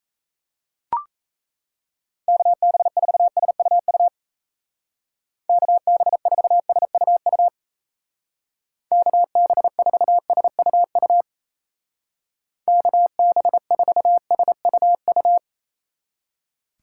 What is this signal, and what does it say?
Morse code, heard as a keyed tone, about 34 wpm, 700 Hz. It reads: KB4SUU KB4SUU KB4SUU KB4SUU
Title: Speed Racing - USA 2x3 Call Signs 28wpm